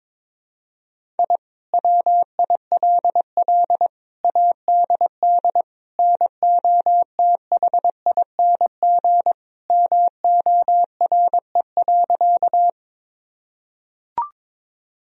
Morse code, heard as a keyed tone, about 22 wpm, 700 Hz. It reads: I WILL ADD NOTHING MORE.
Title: Sentences from Top 1000 Words - Review Part 1 - 22wpm